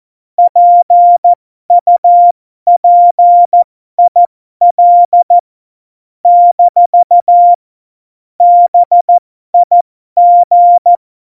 Morse code, heard as a keyed tone, about 14 wpm, spaced as for 12 wpm, 700 Hz. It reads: PUPIL - BIG
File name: Morsekurs / Witze